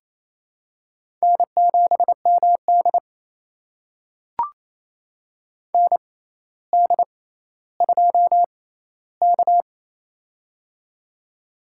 Morse code, heard as a keyed tone, about 28 wpm, spaced as for 14 wpm, 700 Hz. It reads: N7MD ND2K
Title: USA Call Signs - 4 Characters - 6x Character Spacing - 28wpm